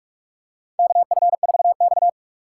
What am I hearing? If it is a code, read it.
Morse code, heard as a keyed tone, about 47 wpm, 700 Hz. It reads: KF4X
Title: USA Call Signs - 4 Characters - 3x Character Spacing - 45wpm